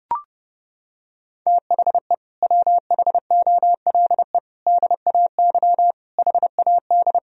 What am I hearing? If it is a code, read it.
Morse code, heard as a keyed tone, about 30 wpm, 700 Hz. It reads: THE WHOLE DAY HAD
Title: Sentences from Top 500 Words and Abbreviations 30wpm